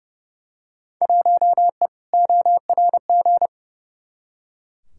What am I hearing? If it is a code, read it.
Morse code, heard as a keyed tone, about 30 wpm, 700 Hz. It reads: 1E ORG
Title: Field Day Exchanges - 30wpm